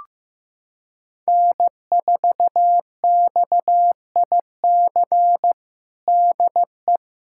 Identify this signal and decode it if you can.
Morse code, heard as a keyed tone, about 15 wpm, 700 Hz. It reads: N4XIC DE
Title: USA Callsign DE Callsign 15wpm